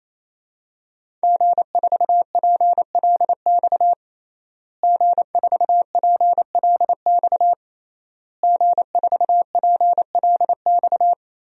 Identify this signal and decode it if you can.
Morse code, heard as a keyed tone, about 27 wpm, 700 Hz. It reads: G4PLX G4PLX G4PLX
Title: DXCC-100 Call Signs - 5 Characters - Repeated 3x - 28wpm